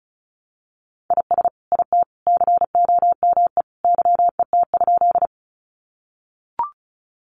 Morse code, heard as a keyed tone, about 34 wpm, 700 Hz. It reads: IS IT COME YET?
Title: Sentences from Top 500 Words 35wpm